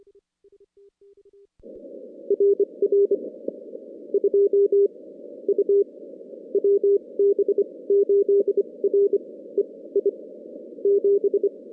Morse code, heard as a keyed tone, about 27 wpm, spaced as for 19 wpm, 410 Hz. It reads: RR 2 U WB8REI 7
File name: contests / CW-SS-k5tr- -2100